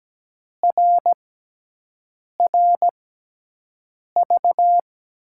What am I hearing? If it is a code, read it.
Morse code, heard as a keyed tone, about 17 wpm, spaced as for 6 wpm, 700 Hz. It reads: RRV
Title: USA Call Signs - 6 Characters - 6x Character Spacing - 17wpm